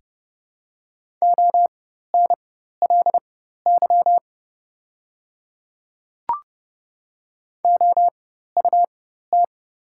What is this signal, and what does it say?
Morse code, heard as a keyed tone, about 30 wpm, spaced as for 14 wpm, 700 Hz. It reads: ONLY OUT
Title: 4x Character Spacing - Top 200 Words - 30wpm